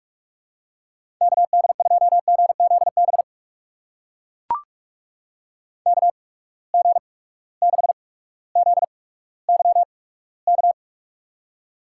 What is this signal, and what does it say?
Morse code, heard as a keyed tone, about 44 wpm, spaced as for 19 wpm, 700 Hz. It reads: KD2GZB KG6ZYK
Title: USA Call Signs - 6 Characters - 8x Character Spacing - 45wpm